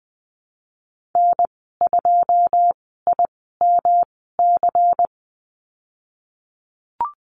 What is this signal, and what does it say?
Morse code, heard as a keyed tone, about 19 wpm, spaced as for 14 wpm, 700 Hz. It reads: N2IMC
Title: 167 Lesson 27g - Number 7 - Call Signs 20wpm